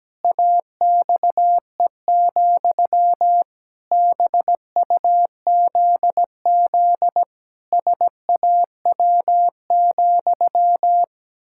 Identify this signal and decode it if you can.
Morse code, heard as a keyed tone, about 17 wpm, 700 Hz. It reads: AXE, BUZZ SAW,
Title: Pangram Challenge v2 - 17wpm